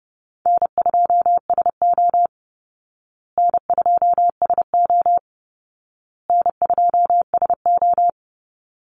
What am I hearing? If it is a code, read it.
Morse code, heard as a keyed tone, about 30 wpm, 700 Hz. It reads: N2SO N2SO N2SO
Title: Call Signs Repeated 2x - 30wpm